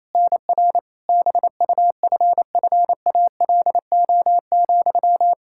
Morse code, heard as a keyed tone, about 27 wpm, 700 Hz. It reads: NR BUFFALO,
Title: K1USN SST QSO Practice 28wpm